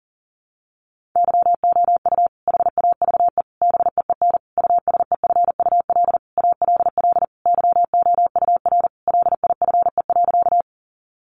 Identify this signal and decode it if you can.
Morse code, heard as a keyed tone, about 40 wpm, 700 Hz. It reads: YOU HAVE BEEN USEFUL ALL YOUR LIFE.
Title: Sentences from Top 700 Words 40wpm